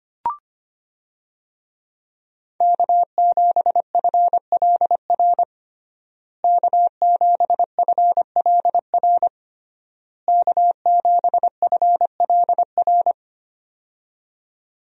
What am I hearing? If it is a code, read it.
Morse code, heard as a keyed tone, about 25 wpm, 700 Hz. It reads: K7FLR K7FLR K7FLR
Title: USA Call Signs - 5 Characters - Repeated 3x - 25wpm